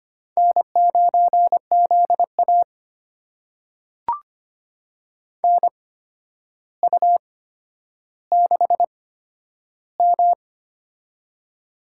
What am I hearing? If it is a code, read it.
Morse code, heard as a keyed tone, about 25 wpm, spaced as for 11 wpm, 700 Hz. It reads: N9ZA NU6M
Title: USA Call Signs - 4 Characters - 8x Character Spacing - 25wpm